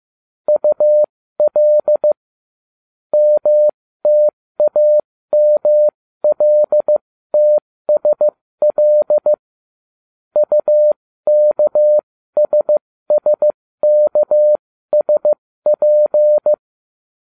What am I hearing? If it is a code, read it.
Morse code, heard as a keyed tone, about 15 wpm, spaced as for 13 wpm, 600 Hz. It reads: UL MTAMLTSL UKSSKSP